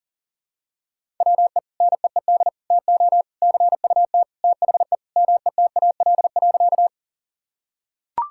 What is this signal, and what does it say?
Morse code, heard as a keyed tone, about 40 wpm, 700 Hz. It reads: WE NEED TO CUT THE METAL.
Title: Sentences from Top 1000 Words - Review Part 6 - 40wpm